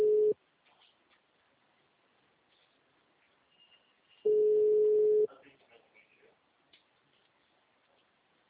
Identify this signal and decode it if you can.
Ringback tone (Europe, 425 Hz), with speech.